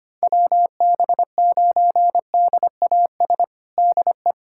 Morse code, heard as a keyed tone, about 25 wpm, 700 Hz. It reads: WB9DAS DE
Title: USA Callsign DE Callsign - 6 Characters - 25wpm